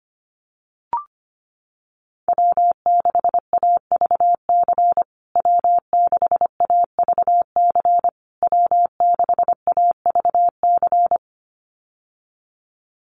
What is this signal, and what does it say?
Morse code, heard as a keyed tone, about 25 wpm, 700 Hz. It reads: W6AVC W6AVC W6AVC
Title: USA Call Signs Repeated 3x 25wpm